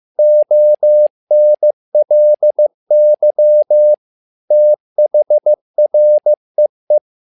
Morse code, heard as a keyed tone, about 15 wpm, 600 Hz. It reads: ONLY THREE
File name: learning center / tower trouble / tower trouble 15wpm